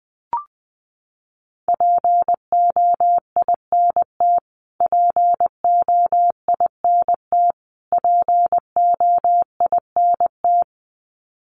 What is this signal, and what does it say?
Morse code, heard as a keyed tone, about 20 wpm, 700 Hz. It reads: POINT POINT POINT
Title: Top 200 Words Repeated 20wpm